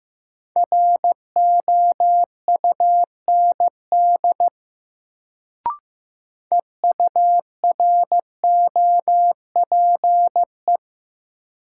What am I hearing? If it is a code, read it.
Morse code, heard as a keyed tone, about 15 wpm, 700 Hz. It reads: ROUND EUROPE
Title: Sets of 5 Words Encore - New Word Drill - 15wpm